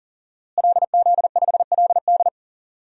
Morse code, heard as a keyed tone, about 37 wpm, 700 Hz. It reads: LZ5LD
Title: DXCC-100 Call Signs - 5 Characters - Repeated 2x - 40wpm